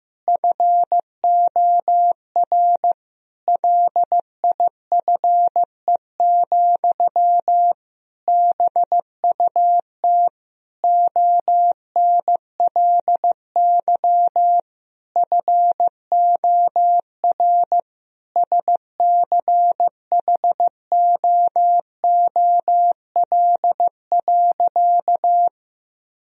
Morse code, heard as a keyed tone, about 15 wpm, 700 Hz. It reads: FOR LIFE, BUT ONLY FOR SCHOOL.
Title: Sentences from Top 1000 Words - Review Part 1 - 15wpm